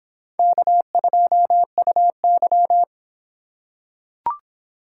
Morse code, heard as a keyed tone, about 24 wpm, 700 Hz. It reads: K2UY